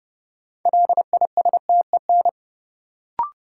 Morse code, heard as a keyed tone, about 31 wpm, 700 Hz. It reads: LISTEN
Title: Sentences from Top 400 Words - New Word Drill - 30wpm